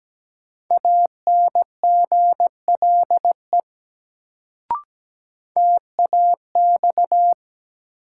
Morse code, heard as a keyed tone, about 17 wpm, 700 Hz. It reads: ANGLE TAX
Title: Sentences from Top 1500 Words - New Word Drill Alt - 17wpm